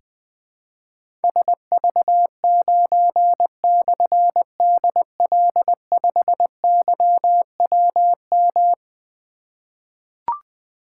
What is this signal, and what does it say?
Morse code, heard as a keyed tone, about 20 wpm, 700 Hz. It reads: SV9/DL5YWM
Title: Difficult Call Signs 20wpm